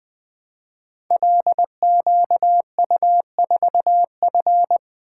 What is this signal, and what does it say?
Morse code, heard as a keyed tone, about 21 wpm, 700 Hz. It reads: LQU4F
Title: Code Groups - 5 Letter:Numbers - 20wpm